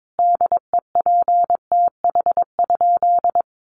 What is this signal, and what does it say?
Morse code, heard as a keyed tone, about 22 wpm, 700 Hz. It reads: DEPTH?
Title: Sentences from Top 1200 Words 22wpm